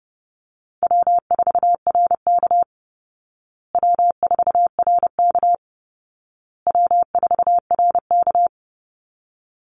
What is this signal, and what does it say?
Morse code, heard as a keyed tone, about 30 wpm, 700 Hz. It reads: W4RK W4RK W4RK
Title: Easy Call Signs Repeated 3x - 30wpm